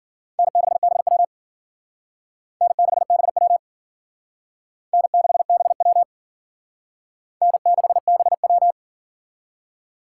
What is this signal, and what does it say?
Morse code, heard as a keyed tone, about 50 wpm, 700 Hz. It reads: N6BW N6BW N6BW N6BW
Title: Speed Racing - USA 1x2 Call Signs 40wpm